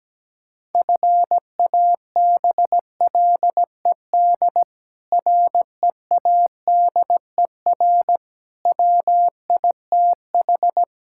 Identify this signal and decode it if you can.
Morse code, heard as a keyed tone, about 17 wpm, 700 Hz. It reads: FABLED READER WITH
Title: Pangram Challenge v2 - 17wpm